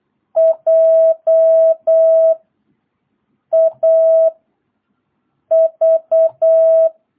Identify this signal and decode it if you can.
Morse code, heard as a keyed tone, about 8 wpm, 650 Hz. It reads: J A V
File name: jh8cbh / 2022cw